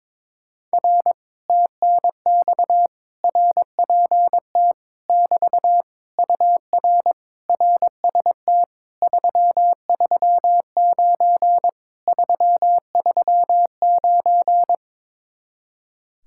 Morse code, heard as a keyed tone, about 22 wpm, 700 Hz. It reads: R TNX RPT <BT> UR RST 339 339
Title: Speed-Racing Common QSO Phrases - 22wpm